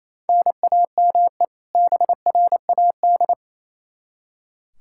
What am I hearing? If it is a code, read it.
Morse code, heard as a keyed tone, about 28 wpm, 700 Hz. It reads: NAME BRAD
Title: Practice QSO - Exchange 1 - Beginning - 28wpm